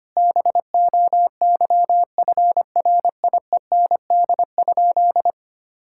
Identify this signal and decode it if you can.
Morse code, heard as a keyed tone, about 25 wpm, 700 Hz. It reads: BOYFRIEND?
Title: Sentences from Top 1000 Words - Review Part 5 - 25wpm